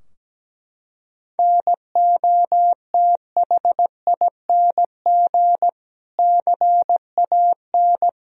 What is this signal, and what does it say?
Morse code, heard as a keyed tone, about 17 wpm, 700 Hz. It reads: NOTHING CAN
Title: Sentences from Top 1000 Words - Review Part 4 - 17wpm